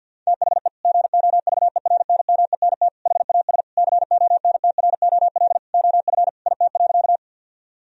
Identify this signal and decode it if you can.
Morse code, heard as a keyed tone, about 50 wpm, 700 Hz. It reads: THE GOVERNMENT HAS CONTROL OF IT.
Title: Sentences from Top 1000 Words - Review Part 3 - 50wpm